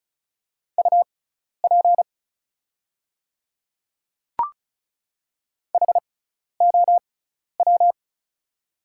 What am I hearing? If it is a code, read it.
Morse code, heard as a keyed tone, about 36 wpm, spaced as for 12 wpm, 690 Hz. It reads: UP HOW
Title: 6x Character Spacing - Top 100 Words - 35wpm